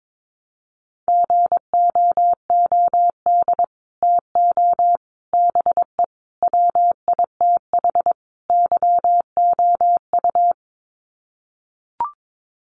Morse code, heard as a keyed tone, about 22 wpm, 700 Hz. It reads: GOOD TO BE WITH YOU
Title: Sets of 5 Words Encore2 22wpm